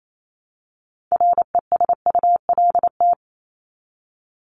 Morse code, heard as a keyed tone, about 28 wpm, 700 Hz. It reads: RESULT
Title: ES:S Suffix - Top 100 Words with Base Words 28wpm